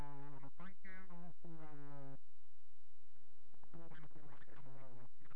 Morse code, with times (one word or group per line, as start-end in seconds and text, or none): none